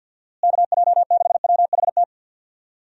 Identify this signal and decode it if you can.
Morse code, heard as a keyed tone, about 49 wpm, 700 Hz. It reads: KJ6WHT
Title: USA Call Signs - 6 Characters - Repeated 3x - 50wpm